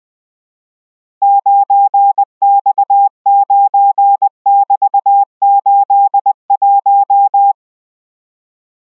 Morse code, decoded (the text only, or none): 9X9<BT>81